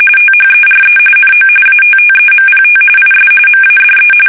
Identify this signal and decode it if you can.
RTTY, about 45.45 baud. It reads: THERE NO ATTEMPT IS MADE
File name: fsk850